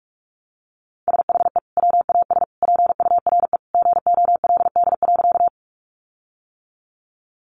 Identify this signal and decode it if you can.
Morse code, heard as a keyed tone, about 44 wpm, 700 Hz. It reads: SHE WAS PURE GOLD.
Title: Sentences from Top 1500 Words 45wpm